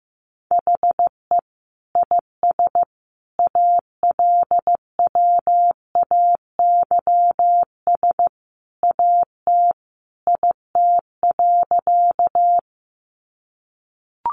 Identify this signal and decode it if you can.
Morse code, heard as a keyed tone, about 15 wpm, 700 Hz. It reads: HE IS ALWAYS AT IT.
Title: Sentences from Top 300 Words 15wpm